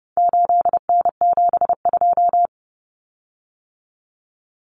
Morse code, heard as a keyed tone, about 30 wpm, 700 Hz. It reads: 8N72
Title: Common Cut Numbers - 4-Digit 30wpm